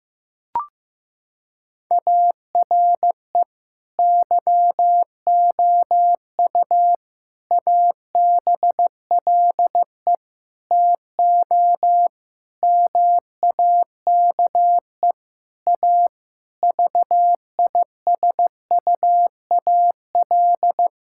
Morse code, heard as a keyed tone, about 15 wpm, 700 Hz. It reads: ARE YOU ABLE TO MAKE A VISUAL
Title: Sentences from Top 1000 Words - Review Part 6 - 15wpm